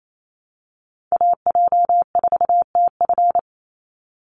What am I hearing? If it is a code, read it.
Morse code, heard as a keyed tone, about 28 wpm, 700 Hz. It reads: AJ4TF
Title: USA Call Signs 28wpm